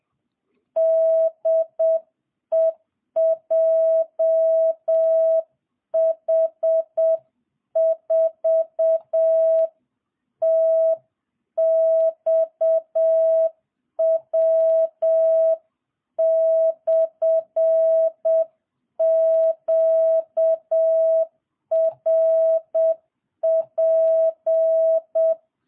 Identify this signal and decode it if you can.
Morse code, heard as a keyed tone, about 7 wpm, 660 Hz. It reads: DE JH4TXW/QRP